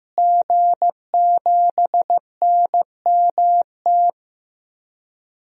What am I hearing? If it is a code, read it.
Morse code, heard as a keyed tone, about 15 wpm, 700 Hz. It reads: G7NMT
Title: DXCC-100 Call Signs - 5 Characters - Repeated 3x - 15wpm